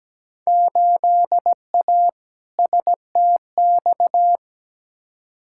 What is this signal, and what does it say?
Morse code, heard as a keyed tone, about 17 wpm, 700 Hz. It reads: 8A STX
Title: Field Day Exchanges Encore - 17wpm